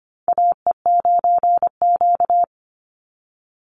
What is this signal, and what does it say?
Morse code, heard as a keyed tone, about 25 wpm, 700 Hz. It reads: AE9Q
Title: NA SOTA Chaser Callsigns 25wpm